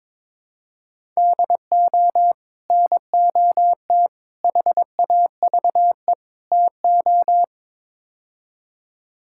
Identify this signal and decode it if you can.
Morse code, heard as a keyed tone, about 22 wpm, 700 Hz. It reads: DO NOT HAVE TO
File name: practice / 22wpm / Build A Phrase - 5-Word Phrases - 22wpm